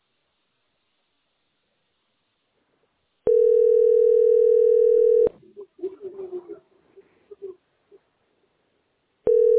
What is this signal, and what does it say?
Ringback tone (North America), 440 + 480 Hz.